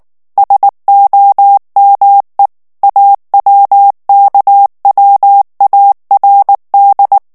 Morse code, heard as a keyed tone, about 19 wpm, 800 Hz. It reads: SOME AWKWARD
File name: ANGLAIS texte clair 2 vitesse 18 mots minute